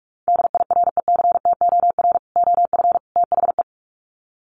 Morse code, heard as a keyed tone, about 44 wpm, 700 Hz. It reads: DIRECTOR OF THE
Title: Sets of 3 Words 45wpm